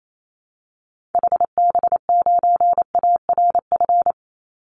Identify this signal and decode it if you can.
Morse code, heard as a keyed tone, about 28 wpm, 700 Hz. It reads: HB9ARF
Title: Call Signs Repeated 3x - 28wpm